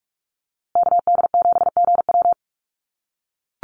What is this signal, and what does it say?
Morse code, heard as a keyed tone, about 44 wpm, 700 Hz. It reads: KD7GW
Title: USA Call Signs Repeated 2x 45wpm